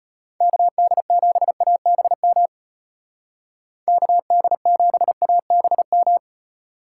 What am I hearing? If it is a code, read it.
Morse code, heard as a keyed tone, about 34 wpm, 700 Hz. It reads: KD7ABM KD7ABM